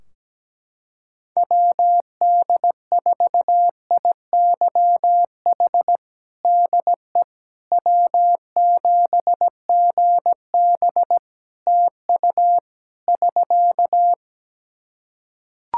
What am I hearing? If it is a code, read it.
Morse code, heard as a keyed tone, about 17 wpm, 700 Hz. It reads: WD4IYH DE W7GB TU <SK>
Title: Practice QSO - Exchange 4 - Ending - 17wpm